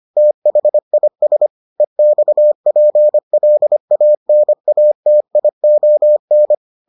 Morse code, heard as a keyed tone, about 25 wpm, 600 Hz. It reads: THIS EXPLANATION